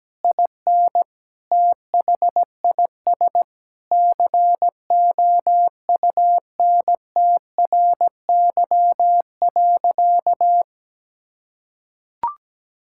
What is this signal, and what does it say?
Morse code, heard as a keyed tone, about 17 wpm, 700 Hz. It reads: IN THIS COUNTRY.
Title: Sentences from Top 1000 Words - Review Part 5 - 17wpm